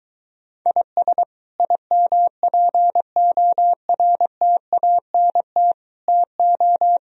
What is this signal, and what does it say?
Morse code, heard as a keyed tone, about 23 wpm, 700 Hz. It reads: IS IMPORTANT TO